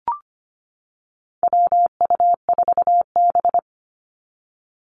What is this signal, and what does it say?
Morse code, heard as a keyed tone, about 25 wpm, 700 Hz. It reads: WU4B
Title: Call Signs - 25wpm